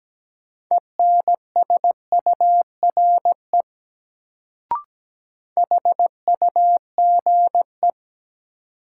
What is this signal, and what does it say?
Morse code, heard as a keyed tone, about 17 wpm, 700 Hz. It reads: ENSURE HUGE
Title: Sentences from Top 1100 Words - New Word Drill - 17wpm